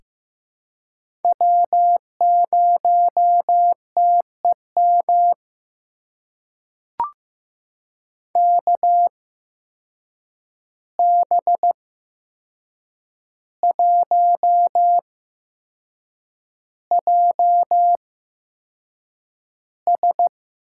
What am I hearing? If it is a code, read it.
Morse code, heard as a keyed tone, about 15 wpm, spaced as for 6 wpm, 700 Hz. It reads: W0TEM KB1JS
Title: USA Call Signs - 5 Characters - 8x Character Spacing - 15wpm